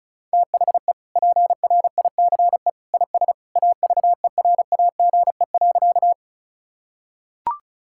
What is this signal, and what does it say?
Morse code, heard as a keyed tone, about 34 wpm, 700 Hz. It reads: THE PRICE IS AVERAGE.
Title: Sentences from Top 1000 Words - Review Part 6 - 35wpm